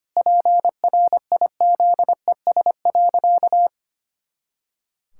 Morse code, heard as a keyed tone, about 25 wpm, 700 Hz. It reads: PRIZES.
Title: Pangram Challenge v2 - 25wpm